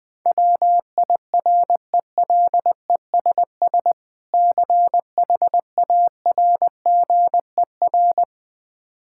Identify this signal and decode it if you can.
Morse code, heard as a keyed tone, about 20 wpm, 700 Hz. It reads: WIRELESS CHARGER
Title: Difficult Words with Hints - 20wpm